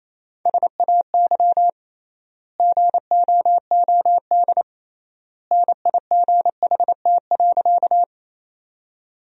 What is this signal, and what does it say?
Morse code, heard as a keyed tone, about 27 wpm, 700 Hz. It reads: SAY GOOD NIGHT.
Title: 3x Word-Spacing - Sentences from Top 300 Words - 28wpm